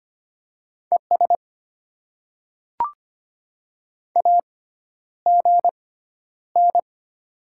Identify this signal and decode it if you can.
Morse code, heard as a keyed tone, about 25 wpm, spaced as for 11 wpm, 700 Hz. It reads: ES AGN
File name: Top 20 QSO Elements - 6x Character Spacing - 25wpm